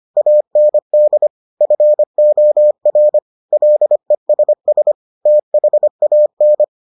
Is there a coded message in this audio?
Yes: Morse code from 0.2 to 6.6 s, heard as a keyed tone, about 25 wpm, 600 Hz. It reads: AND FOR LESS THAN